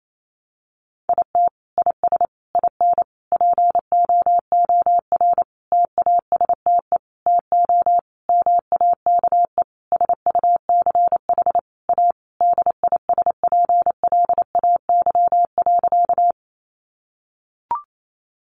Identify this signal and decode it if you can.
Morse code, heard as a keyed tone, about 28 wpm, 700 Hz. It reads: IT IS IN POOR TASTE TO MAKE SUCH A DISPLAY.
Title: Sentences from Top 1200 Words 28wpm